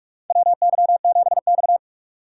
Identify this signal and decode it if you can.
Morse code, heard as a keyed tone, about 46 wpm, 700 Hz. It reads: WY7X
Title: USA Call Signs - 4 Characters - 2x Character Spacing - 45wpm